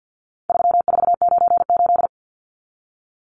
Morse code, heard as a keyed tone, about 50 wpm, 700 Hz. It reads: 3498
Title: Numbers - 4-Digit 50wpm